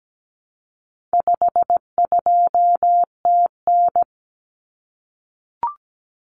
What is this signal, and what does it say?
Morse code, heard as a keyed tone, about 17 wpm, 700 Hz. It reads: 52TN
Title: Common Cut Numbers - 4-Digit 17wpm